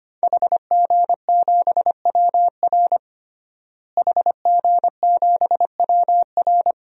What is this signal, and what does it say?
Morse code, heard as a keyed tone, about 25 wpm, 700 Hz. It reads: HG7WR HG7WR
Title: DXCC-100 Call Signs - 5 Characters - Repeated 2x - 25wpm